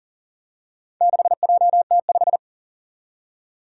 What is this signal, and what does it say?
Morse code, heard as a keyed tone, about 40 wpm, 700 Hz. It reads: 6JT5
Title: Four Challenging Letter-Number Combinations 40wpm